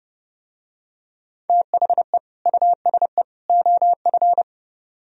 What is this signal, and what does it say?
Morse code, heard as a keyed tone, about 29 wpm, 700 Hz. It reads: THE USE OF